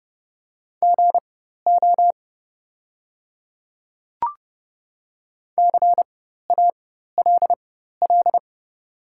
Morse code, heard as a keyed tone, about 30 wpm, spaced as for 14 wpm, 700 Hz. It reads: GO CALL
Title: 4x Character Spacing - Top 100 Words - 30wpm